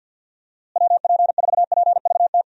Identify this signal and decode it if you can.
Morse code, heard as a keyed tone, about 51 wpm, 700 Hz. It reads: WP4PVT
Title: USA Call Signs - 6 Characters - Repeated 2x - 50wpm